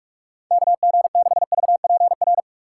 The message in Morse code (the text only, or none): KG6VPR